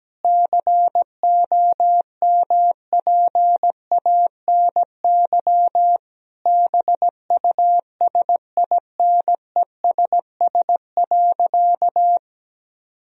Morse code, heard as a keyed tone, about 17 wpm, 700 Hz. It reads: COMPANY BUSINESS.
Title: Sentences from Top 1000 Words - Review Part 3 - 17wpm